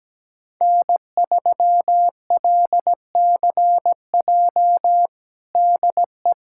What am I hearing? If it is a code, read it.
Morse code, heard as a keyed tone, about 17 wpm, 700 Hz. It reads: N3LCJ DE
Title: USA Callsign DE Callsign 17wpm